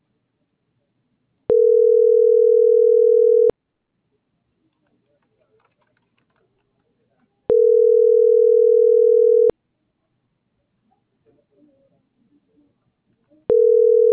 Ringback tone (North America), 440 + 480 Hz.
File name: CENTRAL TERRESTRE / out karis